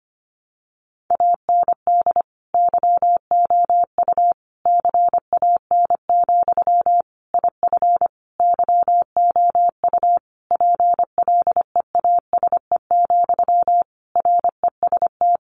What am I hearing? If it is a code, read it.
Morse code, heard as a keyed tone, about 25 wpm, 700 Hz. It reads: AND YOU CAN, IF YOU PLEASE, REST
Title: Sentences from Top 1500 Words 25wpm